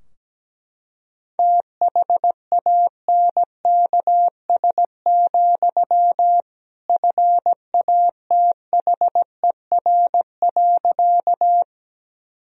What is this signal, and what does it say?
Morse code, heard as a keyed tone, about 17 wpm, 700 Hz. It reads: THANKS, FATHER.
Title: Sentences from Top 1000 Words - Review Part 3 - 17wpm